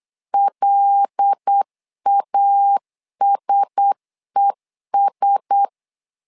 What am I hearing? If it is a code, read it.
Morse code, heard as a keyed tone, about 8 wpm, 800 Hz. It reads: LASES
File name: LEZIONI / 5 LBVC / 5 LBV 8wpm 2837